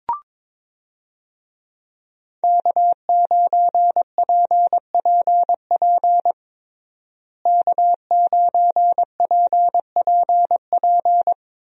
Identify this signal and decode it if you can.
Morse code, heard as a keyed tone, about 22 wpm, 700 Hz. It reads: K9PPP K9PPP
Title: USA Call Signs - 5 Characters - Repeated 3x - 22wpm